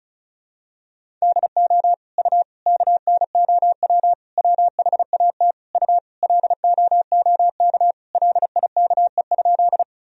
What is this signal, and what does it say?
Morse code, heard as a keyed tone, about 36 wpm, 700 Hz. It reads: DO U KNOW WHAT U LOOK LIKE?
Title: Sentences from Top 100 Words and Abbreviations 35wpm